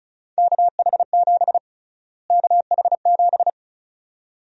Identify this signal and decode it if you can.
Morse code, heard as a keyed tone, about 34 wpm, 700 Hz. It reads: KH7 KH7
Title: USA Call Sign Prefixes - Repeated 2x - 35wpm